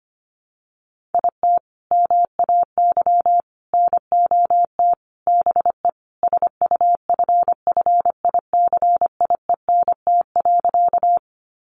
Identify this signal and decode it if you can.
Morse code, heard as a keyed tone, about 25 wpm, 700 Hz. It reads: IT MAY NOT BE SUFFICIENT.
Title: Sentences from Top 1400 Words 25wpm